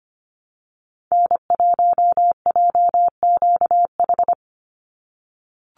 Morse code, heard as a keyed tone, about 25 wpm, 700 Hz. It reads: N1JQH
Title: USA Call Signs 25wpm